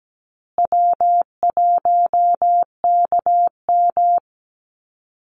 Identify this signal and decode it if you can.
Morse code, heard as a keyed tone, about 17 wpm, 700 Hz. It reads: W1KM
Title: Call Signs Repeated 2x - 17wpm